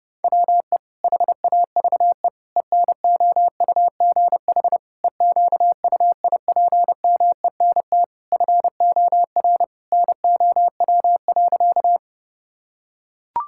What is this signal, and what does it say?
Morse code, heard as a keyed tone, about 31 wpm, 700 Hz. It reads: WE HAVE ENOUGH EQUIPMENT FOR NOW.
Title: Sentences from Top 1000 Words - Review Part 5 - 30wpm